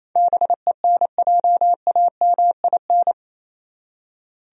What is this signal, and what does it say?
Morse code, heard as a keyed tone, about 28 wpm, 700 Hz. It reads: BENJAMIN
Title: USA Common Names - 28wpm